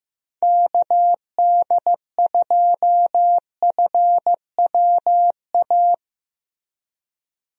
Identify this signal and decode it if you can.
Morse code, heard as a keyed tone, about 15 wpm, 700 Hz. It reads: KD2FWA
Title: USA Call Signs - 6 Characters - 15wpm